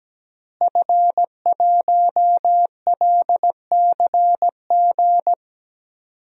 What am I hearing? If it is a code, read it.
Morse code, heard as a keyed tone, about 17 wpm, 700 Hz. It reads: F1LCG
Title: DXCC-100 Call Signs - 5 Characters - Repeated 3x - 17wpm